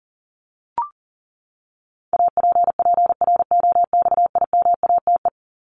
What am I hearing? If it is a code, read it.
Morse code, heard as a keyed tone, about 40 wpm, 700 Hz. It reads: APPROXIMATE
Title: LY Suffix - Top 100 Words with Base Words 40wpm